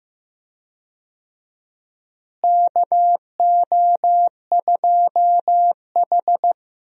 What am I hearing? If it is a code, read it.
Morse code, heard as a keyed tone, about 15 wpm, 700 Hz. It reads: KO2H
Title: USA Call Signs - 4 Characters - Repeated 3x - 15wpm